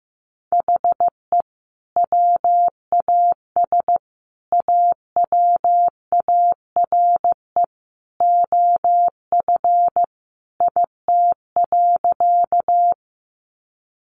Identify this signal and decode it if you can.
Morse code, heard as a keyed tone, about 15 wpm, 700 Hz. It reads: HE WAS AWARE OF IT.
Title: Sentences from Top 1200 Words 15wpm